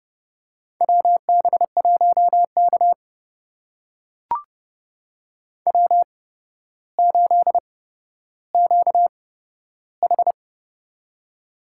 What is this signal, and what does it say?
Morse code, heard as a keyed tone, about 30 wpm, spaced as for 13 wpm, 700 Hz. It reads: WB1K W8QH
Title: USA Call Signs - 4 Characters - 8x Character Spacing - 30wpm